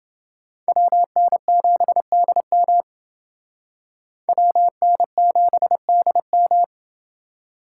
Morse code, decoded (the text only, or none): WN7DM WN7DM